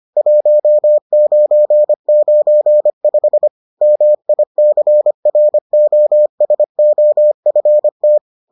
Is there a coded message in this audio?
Yes: Morse code from 0.2 to 8.2 s, heard as a keyed tone, about 25 wpm, 600 Hz. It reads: 1995 MICROSOFT